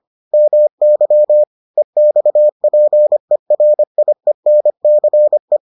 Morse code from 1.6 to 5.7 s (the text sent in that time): EXPERIENCE